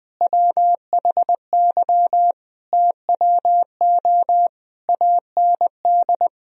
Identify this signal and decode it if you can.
Morse code, heard as a keyed tone, about 20 wpm, 700 Hz. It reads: WHY TWO AND
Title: Sentences from Top 1000 Words - Review Part 1 - 20wpm